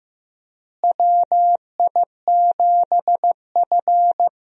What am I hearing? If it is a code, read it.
Morse code, heard as a keyed tone, about 15 wpm, 700 Hz. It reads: WI7F
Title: USA Call Signs - 4 Characters - Repeated 3x - 15wpm